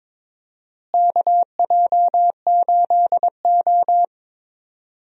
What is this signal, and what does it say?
Morse code, heard as a keyed tone, about 22 wpm, 700 Hz. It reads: KJ8O
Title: USA Call Signs - 4 Characters - Repeated 2x - 22wpm